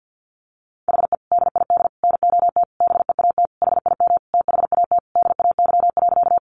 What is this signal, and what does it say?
Morse code, heard as a keyed tone, about 50 wpm, 700 Hz. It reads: HE DID NOT BEAT HIM THAT DAY.
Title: Sentences from Top 1300 Words 50wpm